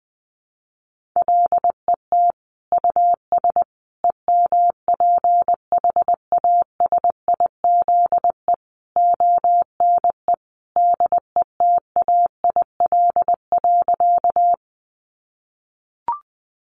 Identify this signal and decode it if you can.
Morse code, heard as a keyed tone, about 20 wpm, 700 Hz. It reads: LET US EMPHASIZE ONE DETAIL.
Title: Sentences from Top 1500 Words 20wpm